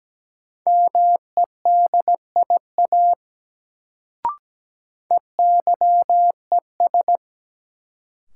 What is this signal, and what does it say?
Morse code, heard as a keyed tone, about 17 wpm, 700 Hz. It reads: MEDIA EYES
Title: Sentences from Top 1000 Words - Review Part 4 - New Word Drill - 17wpm